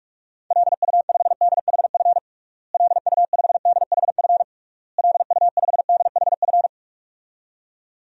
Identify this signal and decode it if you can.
Morse code, heard as a keyed tone, about 46 wpm, 700 Hz. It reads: LU5DHF LU5DHF LU5DHF
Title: DXCC-100 Call Signs - 6 Characters - Repeated 3x - 45wpm